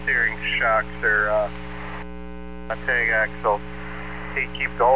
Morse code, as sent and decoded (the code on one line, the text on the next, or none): none